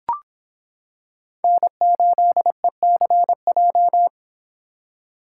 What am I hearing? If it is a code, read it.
Morse code, heard as a keyed tone, about 25 wpm, 700 Hz. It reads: N8ECJ